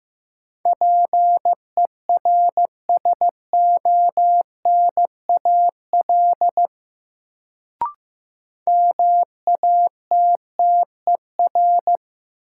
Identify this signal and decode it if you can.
Morse code, heard as a keyed tone, about 15 wpm, 700 Hz. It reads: PERSONAL MATTER
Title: Sentences from Top 700 Words - New Word Drill - 15wpm